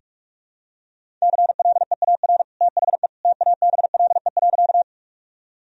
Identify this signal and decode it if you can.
Morse code, heard as a keyed tone, about 42 wpm, 700 Hz. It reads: CLEAR THE TABLE.
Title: Sentences from Top 1000 Words - Review Part 2 - 45wpm